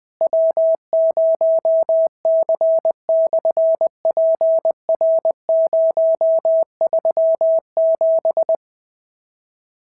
Morse code, heard as a keyed tone, about 20 wpm, 650 Hz. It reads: W0C/PR037
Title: SOTA QSO Practice 20wpm